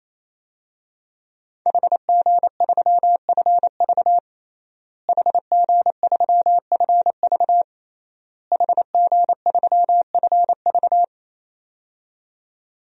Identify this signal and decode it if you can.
Morse code, heard as a keyed tone, about 27 wpm, 700 Hz. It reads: HG3FV HG3FV HG3FV
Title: DXCC-100 Call Signs - 5 Characters - Repeated 3x - 28wpm